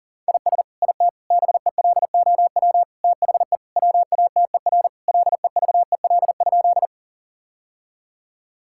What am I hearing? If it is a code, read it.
Morse code, heard as a keyed tone, about 40 wpm, 700 Hz. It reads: IS IT BELOW THE WATER LEVEL?
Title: Sentences from Top 1000 Words - Review Part 5 - 40wpm